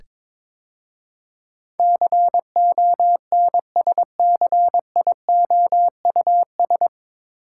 Morse code, heard as a keyed tone, about 22 wpm, 700 Hz. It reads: CONSCIOUS
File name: 9 Letter Words - First 3 Letters Given - 22wpm